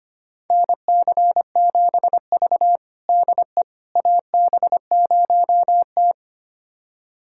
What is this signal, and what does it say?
Morse code, heard as a keyed tone, about 25 wpm, 700 Hz. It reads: NC7V DE AB0T
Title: USA Callsign DE Callsign - 4 Characters - 25wpm